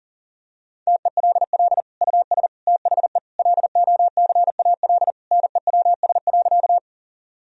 Morse code, heard as a keyed tone, about 40 wpm, 700 Hz. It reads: TELL US THE LOCAL NEWS.
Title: Sentences from Top 1000 Words - Review Part 3 - 40wpm